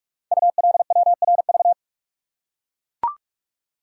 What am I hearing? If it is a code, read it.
Morse code, heard as a keyed tone, about 44 wpm, 700 Hz. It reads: ULWRV
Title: Code Groups - 5 Letters - 45wpm